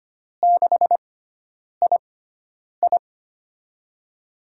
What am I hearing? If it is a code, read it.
Morse code, heard as a keyed tone, about 25 wpm, spaced as for 9 wpm, 700 Hz. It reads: 6II
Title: USA Call Signs - 4 Characters - 6x Character Spacing - 25wpm